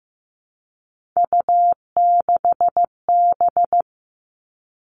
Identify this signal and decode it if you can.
Morse code, heard as a keyed tone, about 15 wpm, 700 Hz. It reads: U6B
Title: All Cut Numbers - 3-Digit 15wpm